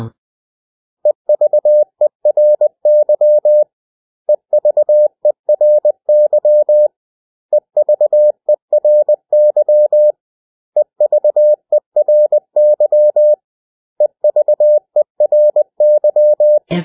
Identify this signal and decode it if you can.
Morse code, heard as a keyed tone, about 20 wpm, 600 Hz. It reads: EVERY EVERY EVERY EVERY EVERY
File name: cw / 100 Common English Words